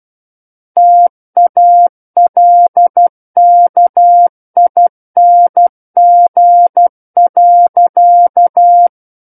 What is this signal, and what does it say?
Morse code, heard as a keyed tone, about 12 wpm, 700 Hz. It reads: TALKING.
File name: Book One / 03-Chapter 3 - 12 WPM